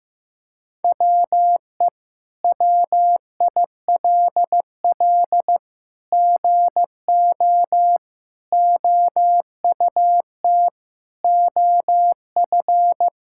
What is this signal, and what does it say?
Morse code, heard as a keyed tone, about 15 wpm, 700 Hz. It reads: WE WILL GO OUT OF
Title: Sentences from Top 1000 Words - Review Part 6 - 15wpm